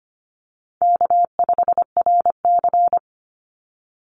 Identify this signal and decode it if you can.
Morse code, heard as a keyed tone, about 25 wpm, 700 Hz. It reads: K5RC
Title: Call Signs - 25wpm